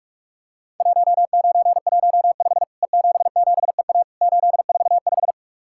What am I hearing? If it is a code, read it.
Morse code, heard as a keyed tone, about 48 wpm, 700 Hz. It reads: 1915 E77EA 845
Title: Worked All Europe DX Contest - QTC Exchange - 45wpm